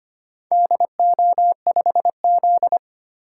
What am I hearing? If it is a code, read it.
Morse code, heard as a keyed tone, about 25 wpm, 700 Hz. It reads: DO5Z
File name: practice / 25wpm / Four Letter-Number Combinations - 25wpm